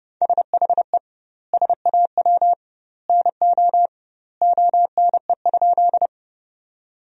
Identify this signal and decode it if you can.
Morse code, heard as a keyed tone, about 32 wpm, 700 Hz. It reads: SHE SAW NO ONE?
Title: 2x Word-Spacing - Sentences from Top 300 Words - 30wpm